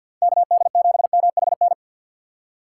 Morse code, heard as a keyed tone, about 52 wpm, 700 Hz. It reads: KD7MHN
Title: USA Call Signs - 2x Character Spacing - 50wpm